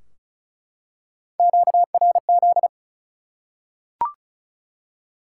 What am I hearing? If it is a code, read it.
Morse code, heard as a keyed tone, about 34 wpm, 700 Hz. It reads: QRZ
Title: Top 180 QSO Elements 35wpm